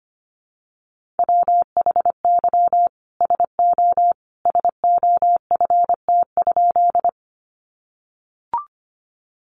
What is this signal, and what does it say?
Morse code, heard as a keyed tone, about 25 wpm, 700 Hz. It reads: WHY SO SOFT?
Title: Sentences from Top 1500 Words 25wpm